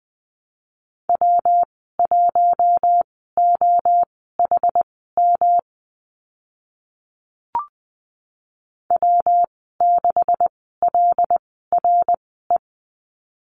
Morse code, heard as a keyed tone, about 20 wpm, spaced as for 15 wpm, 700 Hz. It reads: W1OHM W6LRE
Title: 167 Lesson 27g - Number 7 - Call Signs 20wpm